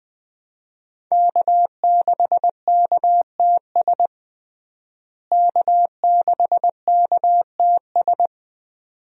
Morse code, heard as a keyed tone, about 20 wpm, 700 Hz. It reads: K6KTS K6KTS
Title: USA Call Signs - 5 Characters - Repeated 2x - 20wpm